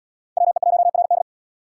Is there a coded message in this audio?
Yes: Morse code from 0.4 to 1.2 s, heard as a keyed tone, about 77 wpm, 700 Hz. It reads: WE1RD